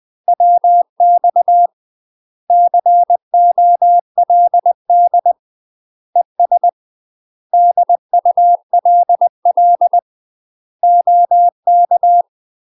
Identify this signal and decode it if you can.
Morse code, heard as a keyed tone, about 20 wpm, 700 Hz. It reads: WX COLD ES DULL OK
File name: UK MASTER QSOs TRAINING / 20WPM / UK MASTER QSOs 0005